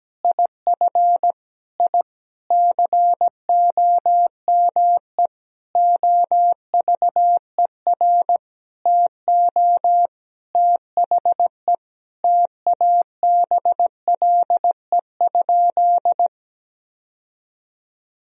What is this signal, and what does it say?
Morse code, heard as a keyed tone, about 17 wpm, 700 Hz. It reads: IF I COME OVER TO THE TABLE?
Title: Sentences from Top 1000 Words - Review Part 2 - 17wpm